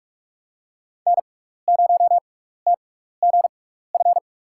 Morse code, heard as a keyed tone, about 42 wpm, spaced as for 15 wpm, 700 Hz. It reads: N0TGF
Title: USA Call Signs - 5 Characters - 6x Character Spacing - 45wpm